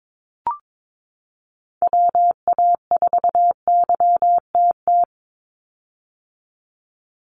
Morse code, heard as a keyed tone, about 22 wpm, 700 Hz. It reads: WA4YTT
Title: USA Call Signs 22wpm